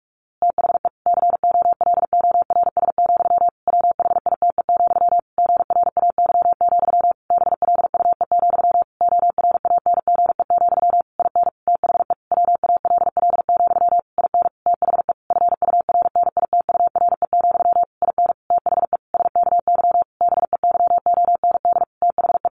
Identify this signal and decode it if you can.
Morse code, heard as a keyed tone, about 44 wpm, 700 Hz. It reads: THE COLORS, WHITE, GRAY, BLUE, ORANGE, IN THE WALL, IN THE FURNITURE, IN THE SKY BEYOND THE
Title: A Sound Of Thunder 45wpm